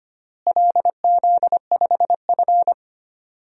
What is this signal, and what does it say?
Morse code, heard as a keyed tone, about 25 wpm, 700 Hz. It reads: LZ5F
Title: DXCC-100 Call Signs - 4 Characters - Repeated 3x - 25wpm